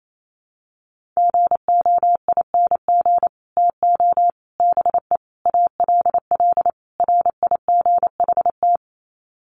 Morse code, heard as a keyed tone, about 28 wpm, 700 Hz. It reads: GOING TO BE ALL RIGHT
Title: Sets of 5 Words Encore2 28wpm